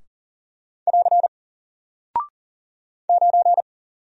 Morse code, heard as a keyed tone, about 40 wpm, 700 Hz. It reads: <AR> 9
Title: Single Character (Letter:Number:Punctuation:Prosign) Alt 40wpm